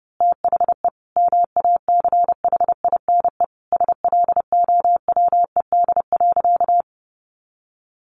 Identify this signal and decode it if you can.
Morse code, heard as a keyed tone, about 30 wpm, 700 Hz. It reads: THE MACHINE SLOWED.
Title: A Sound Of Thunder 30wpm